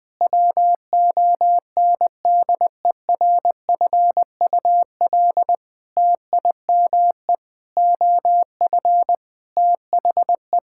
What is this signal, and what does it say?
Morse code, heard as a keyed tone, about 20 wpm, 700 Hz. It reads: WONDERFUL TIME OF THE
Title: Sentences with 8-Word Phrases - Spoken Beginning - 20wpm